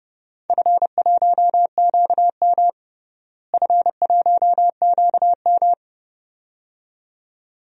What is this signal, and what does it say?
Morse code, heard as a keyed tone, about 32 wpm, 700 Hz. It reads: F1QM F1QM
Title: DXCC-100 Call Signs - 4 Characters - Repeated 2x - 30wpm